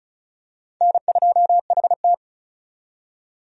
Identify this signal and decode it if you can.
Morse code, heard as a keyed tone, about 36 wpm, 700 Hz. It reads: N2HT
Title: USA Call Signs - 4 Characters - 35wpm